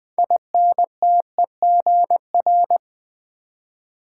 Morse code, heard as a keyed tone, about 20 wpm, 700 Hz. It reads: INTEGR
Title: Build A Word - 3 Letters At A Time - 9-Letter Words - 20wpm